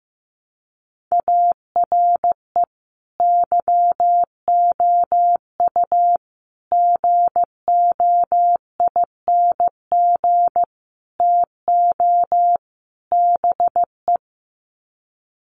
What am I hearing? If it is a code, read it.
Morse code, heard as a keyed tone, about 15 wpm, 700 Hz. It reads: ARE YOU GOING TO BE
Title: Sets of 5 Words Encore 15wpm